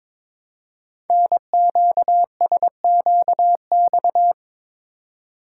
Morse code, heard as a keyed tone, about 22 wpm, 700 Hz. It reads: NQSQX
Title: Code Groups - 5 Letters - 22wpm